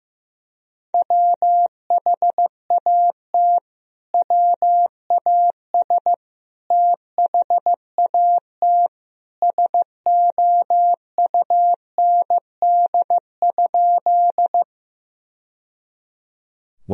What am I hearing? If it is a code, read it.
Morse code, heard as a keyed tone, about 16 wpm, 700 Hz. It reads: WHAT WAS THAT SOUND?
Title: Sentences from Top 1000 Words - Review Part 1 - 15wpm